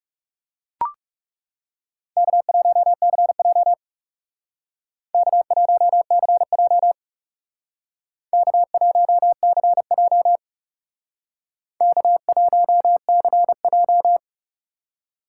Morse code, heard as a keyed tone, about 39 wpm, 700 Hz. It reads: K1CJ K1CJ K1CJ K1CJ